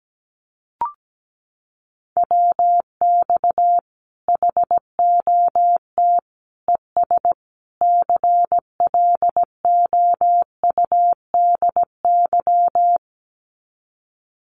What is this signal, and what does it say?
Morse code, heard as a keyed tone, about 17 wpm, 700 Hz. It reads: WX HOT ES CLOUDY
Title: Common QSO Phrases 17wpm